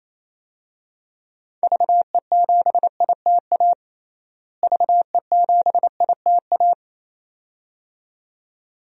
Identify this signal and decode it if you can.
Morse code, heard as a keyed tone, about 28 wpm, 700 Hz. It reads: VE7ITA VE7ITA
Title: DXCC-100 Call Signs - 6 Characters - Repeated 2x - 28wpm